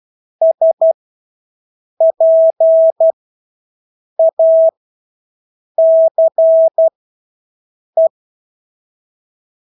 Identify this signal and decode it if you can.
Morse code, heard as a keyed tone, about 12 wpm, spaced as for 6 wpm, 650 Hz. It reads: SPACE
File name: Engelske ord 12-6 wpm